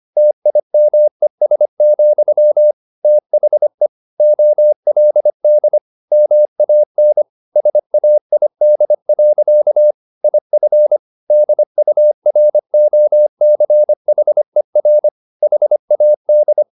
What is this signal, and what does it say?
Morse code, heard as a keyed tone, about 25 wpm, 600 Hz. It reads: TIMES, THE OLD MAN SAID. IF DUROCHER HAD
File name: the old man and the sea part 1 25wpm